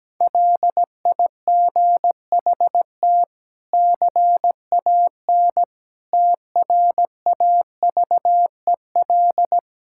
Morse code, heard as a keyed tone, about 17 wpm, 700 Hz. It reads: LIGHT CAN TRAVEL
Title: Sentences from Top 1000 Words - Review Part 6 - 17wpm